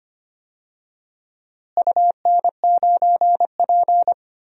Morse code, heard as a keyed tone, about 25 wpm, 700 Hz. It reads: UN9P
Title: DXCC-100 Call Signs - 4 Characters - Repeated 2x - 25wpm